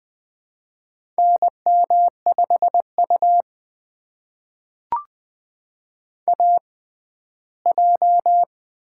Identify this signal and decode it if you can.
Morse code, heard as a keyed tone, about 20 wpm, spaced as for 14 wpm, 700 Hz. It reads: NM5U AJ
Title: USA Call Signs - 4 Characters - 6x Character Spacing - 20wpm